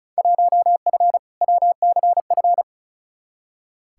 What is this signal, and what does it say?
Morse code, heard as a keyed tone, about 34 wpm, 700 Hz. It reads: 1F WCF
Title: Field Day Exchanges - 35wpm